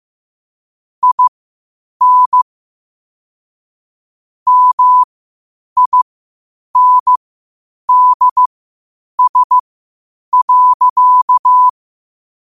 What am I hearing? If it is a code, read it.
Morse code, heard as a keyed tone, about 15 wpm, spaced as for 8 wpm, 1000 Hz. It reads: IN MINDS.
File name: qotdfiles / 8wpm